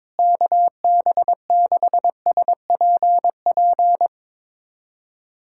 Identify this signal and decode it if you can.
Morse code, heard as a keyed tone, about 22 wpm, 700 Hz. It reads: KB6SPP
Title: USA Callsign DE Callsign 22wpm